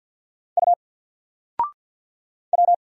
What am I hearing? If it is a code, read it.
Morse code, heard as a keyed tone, about 50 wpm, 700 Hz. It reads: U W
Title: Single Character (Letter:Number:Punctuation:Prosign) 50wpm